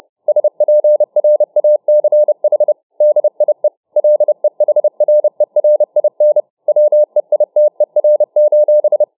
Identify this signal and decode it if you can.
Morse code, heard as a keyed tone, about 30 wpm, 600 Hz. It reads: SPRACH DIE LEHRERIN WEITER:
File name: dk5ke / Morsetanz 30WpM